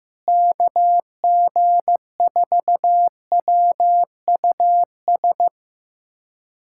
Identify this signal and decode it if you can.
Morse code, heard as a keyed tone, about 15 wpm, 700 Hz. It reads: KG4WUS
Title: USA Callsign DE Callsign - 6 Characters - 15wpm